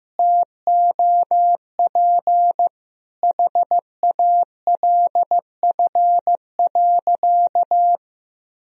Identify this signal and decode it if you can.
Morse code, heard as a keyed tone, about 15 wpm, 700 Hz. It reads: TOP HALF.
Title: Sentences from Top 1000 Words - Review Part 2 - 15wpm